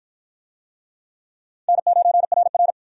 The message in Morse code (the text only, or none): N9RR